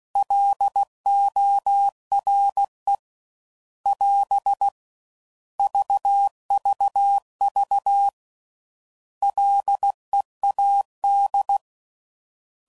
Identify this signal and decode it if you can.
Morse code, heard as a keyed tone, about 16 wpm, 800 Hz. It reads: LORE <AS> VVV LEAD
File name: Prefixes / By First Letter / Ls